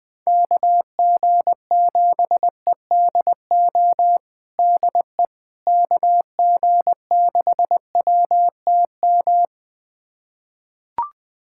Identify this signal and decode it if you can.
Morse code, heard as a keyed tone, about 21 wpm, 700 Hz. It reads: KG7EDO DE KG6WTM
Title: USA Callsign DE Callsign 20wpm